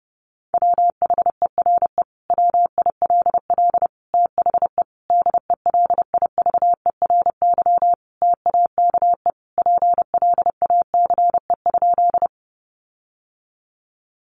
Morse code, heard as a keyed tone, about 30 wpm, 700 Hz. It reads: WHERE WILL THE DELIVERY TAKE PLACE?
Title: Sentences from Top 1500 Words 30wpm